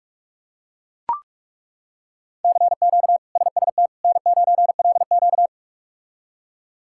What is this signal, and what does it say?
Morse code, heard as a keyed tone, about 41 wpm, 700 Hz. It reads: CQ SST N9LQ
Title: K1USN SST QSO Practice - 45wpm